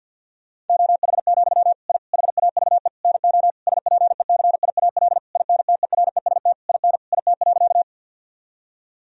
Morse code, heard as a keyed tone, about 50 wpm, 700 Hz. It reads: OH, I HAVE NO SPECIAL INTEREST IN IT.
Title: Sentences from Top 1000 Words - Review Part 3 - 50wpm